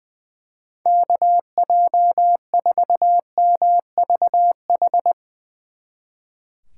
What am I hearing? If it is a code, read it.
Morse code, heard as a keyed tone, about 21 wpm, 700 Hz. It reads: KJ4MVH
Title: Speed Racing - USA Call Signs 20wpm